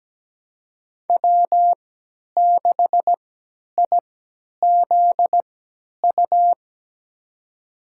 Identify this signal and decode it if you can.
Morse code, heard as a keyed tone, about 17 wpm, spaced as for 10 wpm, 700 Hz. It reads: W6IZU
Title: USA Call Signs - 5 Characters - 3x Character Spacing - 17wpm